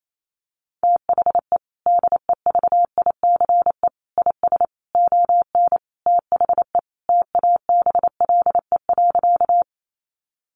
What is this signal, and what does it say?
Morse code, heard as a keyed tone, about 28 wpm, 700 Hz. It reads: THE DEVICE IS ON THE TABLE.
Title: Sentences from Top 1500 Words 28wpm